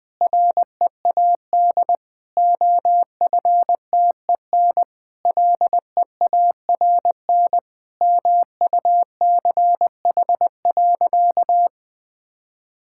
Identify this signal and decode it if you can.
Morse code, heard as a keyed tone, about 20 wpm, 700 Hz. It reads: READ OFTEN LEARN MUCH.
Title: Sentences with 2-Word Phrases - 20wpm